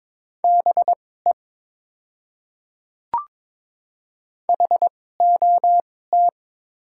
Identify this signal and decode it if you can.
Morse code, heard as a keyed tone, about 22 wpm, spaced as for 16 wpm, 700 Hz. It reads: BE HOT
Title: 2x Character Spacing - Top 100 Words - 22wpm